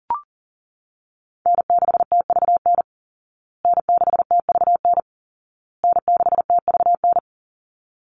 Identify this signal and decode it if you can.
Morse code, heard as a keyed tone, about 40 wpm, 700 Hz. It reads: N6TVN N6TVN N6TVN
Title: Call Signs Repeated 2x - 40wpm